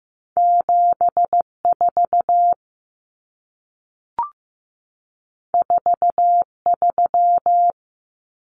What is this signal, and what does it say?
Morse code, heard as a keyed tone, about 15 wpm, 700 Hz. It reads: 74 43
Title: Common Cut Numbers - 2-Digit 15wpm